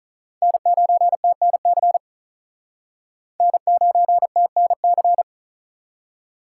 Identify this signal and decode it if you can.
Morse code, heard as a keyed tone, about 45 wpm, 700 Hz. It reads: N9TNC N9TNC